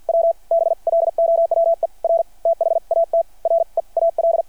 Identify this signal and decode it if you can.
Morse code, heard as a keyed tone, about 52 wpm, spaced as for 41 wpm, 600 Hz. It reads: W BLOWER THAT REAL
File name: archived / CRSnet